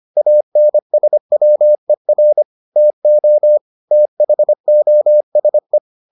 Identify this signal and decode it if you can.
Morse code, heard as a keyed tone, about 25 wpm, 600 Hz. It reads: ANSWER TO THOSE